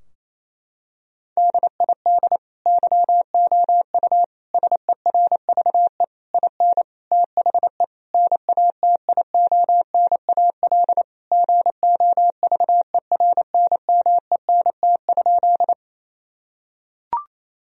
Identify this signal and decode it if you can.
Morse code, heard as a keyed tone, about 28 wpm, 700 Hz. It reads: DID YOU SERVE IN THE NATIONAL GOVERNMENT?
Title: Sentences from Top 1000 Words - Review Part 4 - 28wpm